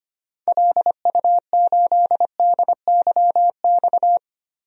Morse code, heard as a keyed tone, about 25 wpm, 700 Hz. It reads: LU8DYX
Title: DXCC-100 Call Signs - 6 Characters - Repeated 3x - 25wpm